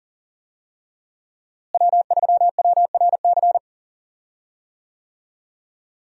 Morse code, heard as a keyed tone, about 39 wpm, 700 Hz. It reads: W3WRC
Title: USA Call Signs - 5 Characters - Repeated 2x - 40wpm